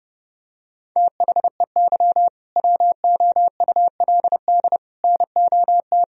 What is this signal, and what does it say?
Morse code, heard as a keyed tone, about 30 wpm, 700 Hz. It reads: THEY WOULD NOT
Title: Sentences from Top 1000 Words - Review Part 3 - 30wpm